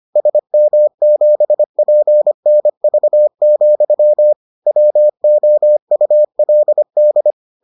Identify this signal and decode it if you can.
Morse code, heard as a keyed tone, about 25 wpm, 600 Hz. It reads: SM7PNV, WOULD